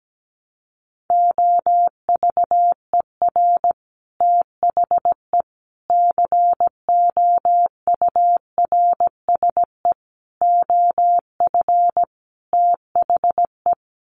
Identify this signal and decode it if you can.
Morse code, heard as a keyed tone, about 17 wpm, 700 Hz. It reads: OVER THE COURSE OF THE
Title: Sets of 5 Words 17wpm